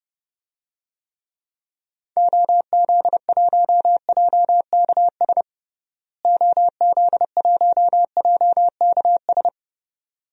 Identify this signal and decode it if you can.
Morse code, heard as a keyed tone, about 30 wpm, 700 Hz. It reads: OZ1JKS OZ1JKS
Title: DXCC-100 Call Signs - 6 Characters - Repeated 2x - 30wpm